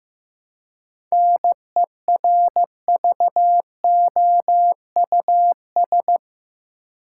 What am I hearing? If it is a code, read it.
Morse code, heard as a keyed tone, about 15 wpm, 700 Hz. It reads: NERVOUS
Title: Seven Letter Words - 15wpm